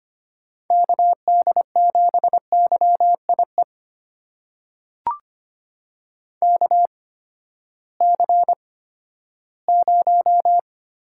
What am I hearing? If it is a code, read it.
Morse code, heard as a keyed tone, about 25 wpm, spaced as for 14 wpm, 700 Hz. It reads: KD7YIE KC0
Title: USA Call Signs - 6 Characters - 8x Character Spacing - 25wpm